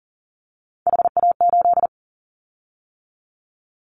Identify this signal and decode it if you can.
Morse code, heard as a keyed tone, about 39 wpm, 700 Hz. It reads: HA8
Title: Callsign Prefixes for Top 25 DXCC - 40wpm